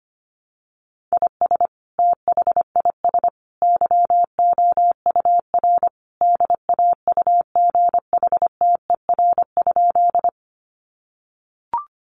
Morse code, heard as a keyed tone, about 25 wpm, 700 Hz. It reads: IS THIS YOUR DAUGHTER?
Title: Sentences from Top 1500 Words 25wpm